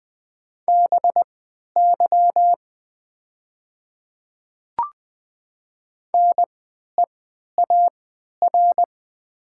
Morse code, heard as a keyed tone, about 20 wpm, spaced as for 11 wpm, 700 Hz. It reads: BY NEAR
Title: 3x Character Spacing - Top 300 Words - 20wpm